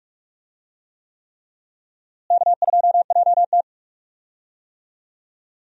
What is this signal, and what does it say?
Morse code, heard as a keyed tone, about 47 wpm, 700 Hz. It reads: K2JT
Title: USA Call Signs - 4 Characters - Repeated 3x - 45wpm